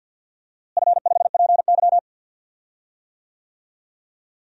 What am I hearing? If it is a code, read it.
Morse code, heard as a keyed tone, about 53 wpm, 700 Hz. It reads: F5PY
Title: DXCC-100 Call Signs - 4 Characters - Repeated 3x - 50wpm